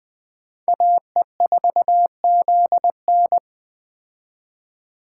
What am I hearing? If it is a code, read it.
Morse code, heard as a keyed tone, about 20 wpm, 700 Hz. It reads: AE4ZN